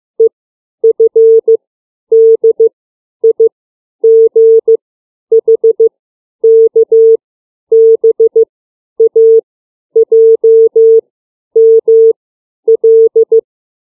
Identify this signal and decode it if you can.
Morse code, heard as a keyed tone, about 15 wpm, 450 Hz. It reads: E F D I G H K B A J M L
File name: CW / alphabet 7